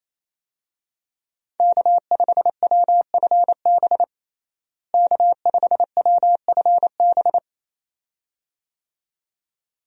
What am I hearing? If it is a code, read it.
Morse code, heard as a keyed tone, about 28 wpm, 700 Hz. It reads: K5WFB K5WFB
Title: USA Call Signs - 5 Characters - Repeated 2x - 28wpm